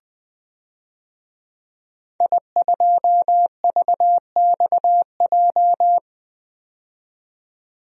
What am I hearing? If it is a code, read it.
Morse code, heard as a keyed tone, about 21 wpm, 700 Hz. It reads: I2VXJ
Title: DXCC-100 Call Signs - 5 Characters - 20wpm